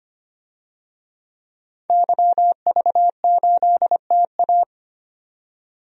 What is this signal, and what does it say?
Morse code, heard as a keyed tone, about 25 wpm, 700 Hz. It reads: YV8TA
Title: DXCC-100 Call Signs - 5 Characters - 25wpm